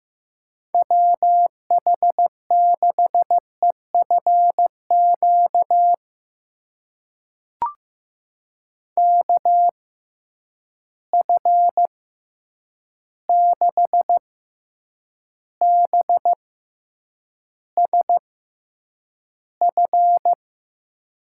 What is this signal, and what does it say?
Morse code, heard as a keyed tone, about 15 wpm, spaced as for 8 wpm, 700 Hz. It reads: WH6EFQ KF6BSF
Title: USA Call Signs - 6 Characters - 6x Character Spacing - 15wpm